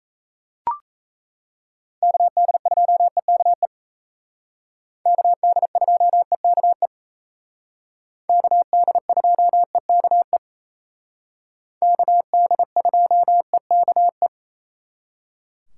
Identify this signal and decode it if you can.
Morse code, heard as a keyed tone, about 35 wpm, 700 Hz. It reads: KD2EKE KD2EKE KD2EKE KD2EKE